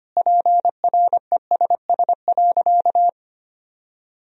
Morse code, heard as a keyed tone, about 25 wpm, 700 Hz. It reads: PRESS.
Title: Sentences from Top 1000 Words - Review Part 1 - 25wpm